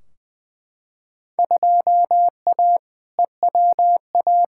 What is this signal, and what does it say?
Morse code, heard as a keyed tone, about 20 wpm, 700 Hz. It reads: 2A EWA
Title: Field Day Exchanges - 20wpm